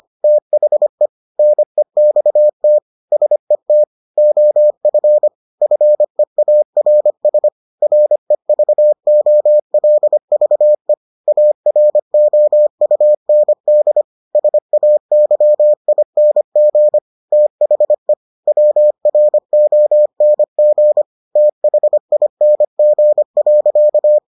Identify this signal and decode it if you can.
Morse code, heard as a keyed tone, about 25 wpm, 600 Hz. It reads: THE NEXT SET OF FEARS REVOLVE AROUND SAYING THE WRONG THING.